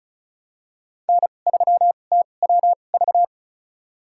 Morse code, heard as a keyed tone, about 36 wpm, spaced as for 25 wpm, 700 Hz. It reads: N3TWV
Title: USA Call Signs - 2x Character Spacing - 35wpm